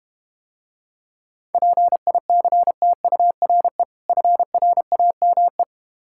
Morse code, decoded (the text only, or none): PICTURE FRAME